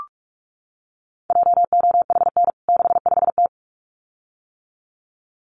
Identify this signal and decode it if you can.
Morse code, heard as a keyed tone, about 44 wpm, 700 Hz. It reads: JOHN 65T
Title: CWT Exchange 45wpm